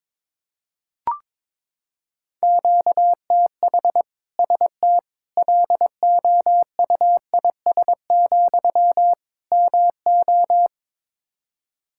Morse code, heard as a keyed tone, about 21 wpm, 700 Hz. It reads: QTH ST LOUIS, MO
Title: Practice QSO - Exchange 1 - Beginning - 22wpm